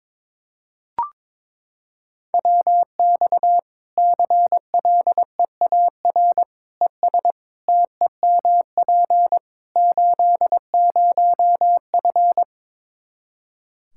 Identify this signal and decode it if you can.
Morse code, heard as a keyed tone, about 22 wpm, 700 Hz. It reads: WX CLEAR ES TEMP 80F
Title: K1USN SST QSO Practice 22wpm